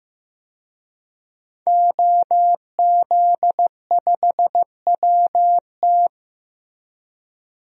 Morse code, heard as a keyed tone, about 15 wpm, 700 Hz. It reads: OZ5WT
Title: DXCC-100 Call Signs - 5 Characters - Repeated 3x - 15wpm